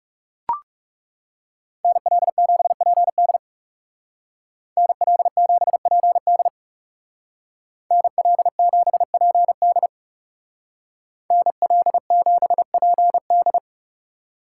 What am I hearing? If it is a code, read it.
Morse code, heard as a keyed tone, about 36 wpm, 700 Hz. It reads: NL7PD NL7PD NL7PD NL7PD
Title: Speed Racing - USA Call Signs 30wpm